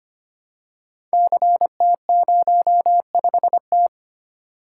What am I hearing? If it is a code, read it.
Morse code, heard as a keyed tone, about 25 wpm, 700 Hz. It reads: CT05T
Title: Code Groups - 5 Letter:Numbers - 25wpm